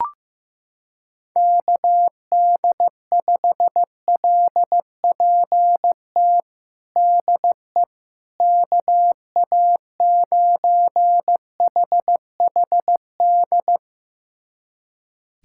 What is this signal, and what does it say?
Morse code, heard as a keyed tone, about 15 wpm, 700 Hz. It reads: KD5LPT DE KA9HHD
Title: USA Callsign DE Callsign 15wpm